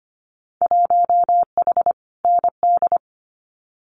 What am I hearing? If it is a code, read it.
Morse code, heard as a keyed tone, about 25 wpm, 700 Hz. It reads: 1H ND
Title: Winter Field Day Exchanges 25wpm